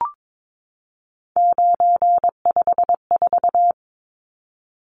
Morse code, heard as a keyed tone, about 22 wpm, 700 Hz. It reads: 954
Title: Numbers - 3-Digit 22wpm